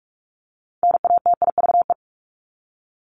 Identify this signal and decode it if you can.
Morse code, heard as a keyed tone, about 44 wpm, 700 Hz. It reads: NATIVE
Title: Top 1751-2000 Words 45wpm